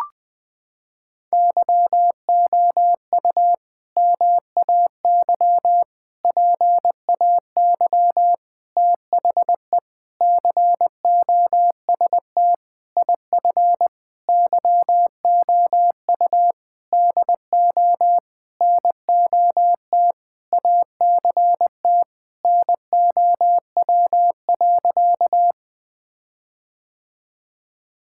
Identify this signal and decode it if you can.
Morse code, heard as a keyed tone, about 21 wpm, 700 Hz. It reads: YOU MAY PAY THE COST IF YOU DO NOT ACT NOW.
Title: Sentences with 8-Word Phrases - 20wpm